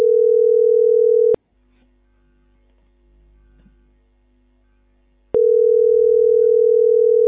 Ringback tone (North America), 440 + 480 Hz.